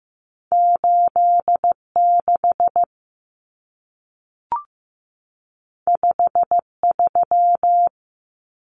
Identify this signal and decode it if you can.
Morse code, heard as a keyed tone, about 15 wpm, 700 Hz. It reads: 86 53
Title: Common Cut Numbers - 2-Digit 15wpm